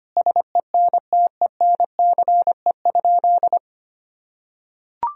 Morse code, heard as a keyed tone, about 25 wpm, 700 Hz. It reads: SENTENCE?
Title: Sentences from Top 1000 Words - Review Part 1 - 25wpm